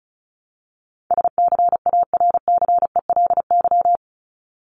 Morse code, heard as a keyed tone, about 34 wpm, 700 Hz. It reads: SCARCELY
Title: LY Suffix - Top 100 Words with Base Words 35wpm